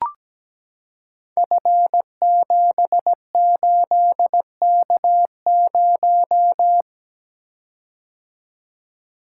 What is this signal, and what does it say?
Morse code, heard as a keyed tone, about 17 wpm, 700 Hz. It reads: F78K0
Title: Code Groups - 5 Letter:Numbers - 17wpm